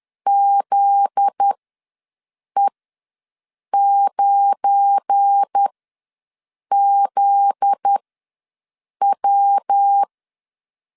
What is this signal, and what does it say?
Morse code, heard as a keyed tone, about 10 wpm, spaced as for 6 wpm, 800 Hz. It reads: ZE9ZW
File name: LEZIONI / 7 GF9Z / 7 GF9Z ws 10wpm 3919